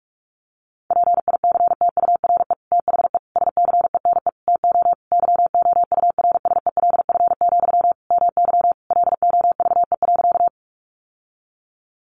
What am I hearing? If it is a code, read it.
Morse code, heard as a keyed tone, about 44 wpm, 700 Hz. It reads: PICTURE THE SCENE TO YOURSELF, MY LOVE.
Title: Sentences from Top 1100 Words 45wpm